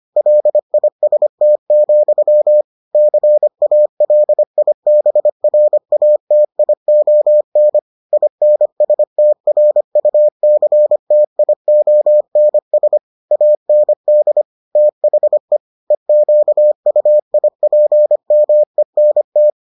Morse code, heard as a keyed tone, about 25 wpm, 600 Hz. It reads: LIST, CALIBRATION INSTRUCTIONS AND THE EQUIPMENT